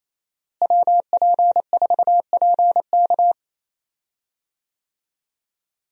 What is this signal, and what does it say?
Morse code, heard as a keyed tone, about 27 wpm, 700 Hz. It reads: WP4PK
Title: USA Call Signs - 5 Characters - Repeated 2x - 28wpm